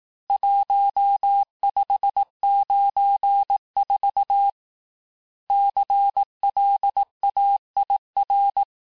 Morse code, heard as a keyed tone, about 18 wpm, 800 Hz. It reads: 1594 CLAIR
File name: cours-du-lundi files / cours-1594--teste-clair-vitesse-17mpm-puis-22mpm